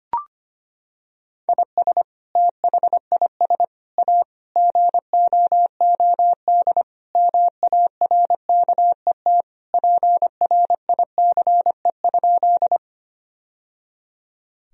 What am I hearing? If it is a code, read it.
Morse code, heard as a keyed tone, about 25 wpm, 700 Hz. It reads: IS THIS A GOOD MARKET PRICE?
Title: Sentences from Top 1000 Words - Review Part 3 - 25wpm